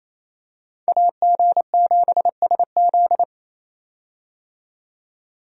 Morse code, heard as a keyed tone, about 28 wpm, 700 Hz. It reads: AG7SZ
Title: USA Call Signs - 5 Characters - 28wpm